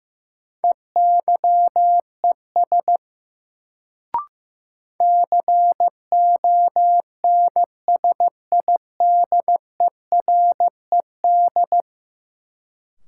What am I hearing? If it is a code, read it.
Morse code, heard as a keyed tone, about 15 wpm, 700 Hz. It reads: EYES CONSIDERED
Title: Sets of 5 Words - New Word Drill - 15wpm